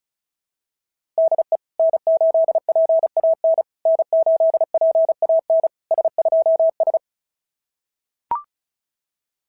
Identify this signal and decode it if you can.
Morse code, heard as a keyed tone, about 36 wpm, 690 Hz. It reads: DE N8PAN N8PAN S2S
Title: SOTA QSO Practice 35wpm